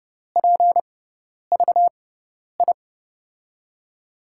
Morse code, heard as a keyed tone, about 29 wpm, spaced as for 10 wpm, 700 Hz. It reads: PVI
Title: USA Call Signs - 6x Character Spacing - 30wpm